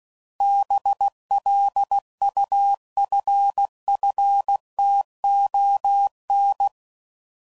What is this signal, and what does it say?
Morse code, heard as a keyed tone, about 16 wpm, 800 Hz. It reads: BLUFFTON
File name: CW Code Groups @15WPM / fromSM's / Cities